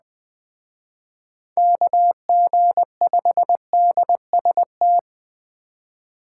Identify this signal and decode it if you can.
Morse code, heard as a keyed tone, about 21 wpm, 700 Hz. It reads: KG5DST
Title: Speed Racing - USA 2x3 Call Signs 17wpm